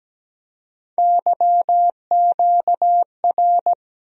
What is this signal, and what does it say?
Morse code, heard as a keyed tone, about 17 wpm, 700 Hz. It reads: YQR
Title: USA Call Sign Postfixes - 3 Letters 17wpm